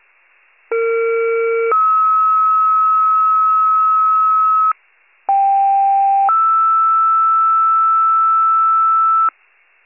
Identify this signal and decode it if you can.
Two-tone page, heard as a one-second tone, then a three-second tone, about 460 then 1290 Hz, about 790 then 1320 Hz. It is heard from 0.7 to 9.3 s.
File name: pager / 21 RESCUE / wcalert